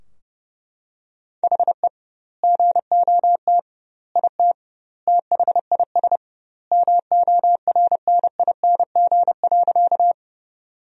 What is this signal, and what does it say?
Morse code, heard as a keyed tone, about 32 wpm, 700 Hz. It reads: HE GOT IT THIS MORNING.
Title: 2x Word-Spacing - Sentences from Top 400 Words - 30wpm